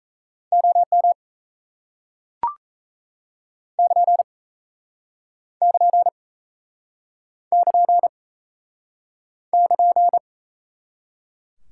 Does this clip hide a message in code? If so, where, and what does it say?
Yes: Morse code from 0.5 to 10.2 s, heard as a keyed tone, about 39 wpm, 700 Hz. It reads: OM <KN> <KN> <KN> <KN>